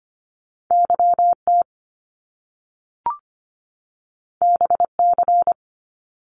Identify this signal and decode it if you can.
Morse code, heard as a keyed tone, about 25 wpm, 700 Hz. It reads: YT BC
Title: Canadian Provinces and Territories 25wpm